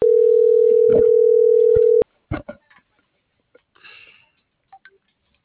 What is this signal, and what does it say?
Ringback tone (North America), 440 + 480 Hz.